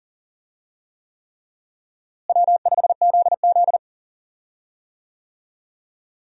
Morse code, heard as a keyed tone, about 40 wpm, 700 Hz. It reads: W5ZZ
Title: USA Call Signs - 4 Characters - Repeated 3x - 40wpm